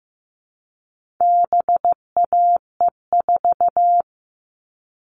Morse code, heard as a keyed tone, about 15 wpm, 700 Hz. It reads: BAE4
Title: All Cut Numbers - 4-Digit 15wpm